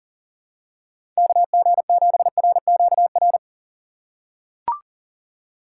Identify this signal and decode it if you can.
Morse code, heard as a keyed tone, about 40 wpm, 700 Hz. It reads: KG7RQR
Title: USA Call Signs - 6x Character Spacing - 40wpm